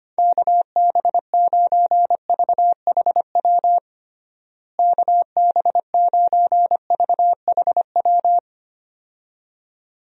Morse code, heard as a keyed tone, about 25 wpm, 700 Hz. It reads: KB9VHW KB9VHW
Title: USA Call Signs - 6 Characters - Repeated 2x - 25wpm